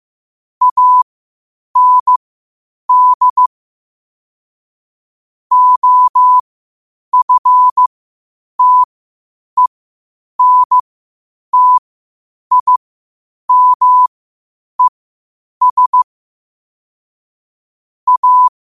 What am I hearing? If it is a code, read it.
Morse code, heard as a keyed tone, about 15 wpm, spaced as for 8 wpm, 1000 Hz. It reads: AND OFTENTIMES A
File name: qotdfiles / 8wpm